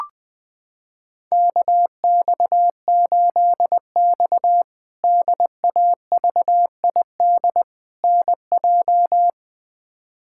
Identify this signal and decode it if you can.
Morse code, heard as a keyed tone, about 20 wpm, 700 Hz. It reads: KX8X DAVID NJ
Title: K1USN SST QSO Practice - 20wpm